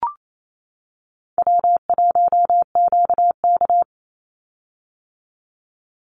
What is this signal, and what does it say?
Morse code, heard as a keyed tone, about 28 wpm, 700 Hz. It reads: W1QK
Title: Easy Call Signs - 28wpm